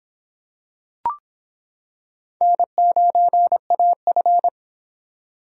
Morse code, heard as a keyed tone, about 25 wpm, 700 Hz. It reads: N9AF